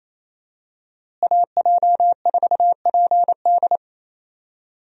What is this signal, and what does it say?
Morse code, heard as a keyed tone, about 28 wpm, 700 Hz. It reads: AJ4PD
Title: USA Call Signs - 5 Characters - Repeated 2x - 28wpm